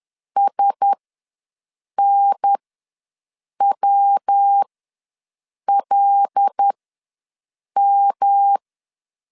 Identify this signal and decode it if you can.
Morse code, heard as a keyed tone, about 10 wpm, spaced as for 6 wpm, 800 Hz. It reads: SNWLM
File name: LEZIONI / 5 LBVC / 5 L ws 10wpm 3636